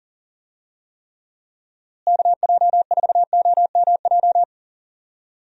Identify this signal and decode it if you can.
Morse code, heard as a keyed tone, about 40 wpm, 700 Hz. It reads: KJ4OMJ
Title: USA Call Signs - 6 Characters - Repeated 2x - 40wpm